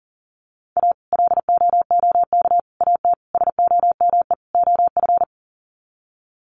Morse code, heard as a keyed tone, about 40 wpm, 700 Hz. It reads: A LOOK AT SOME OF
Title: Sets of 5 Words Encore2 40wpm